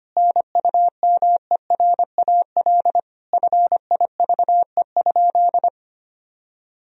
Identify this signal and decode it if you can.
Morse code, heard as a keyed tone, about 25 wpm, 700 Hz. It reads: NUMERAL FIVE?
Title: Sentences from Top 1000 Words - Review Part 2 - 25wpm